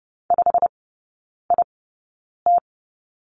Morse code, heard as a keyed tone, about 30 wpm, spaced as for 9 wpm, 700 Hz. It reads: 5IT
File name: Lesson 16e - Letter c - Three Characters 30wpm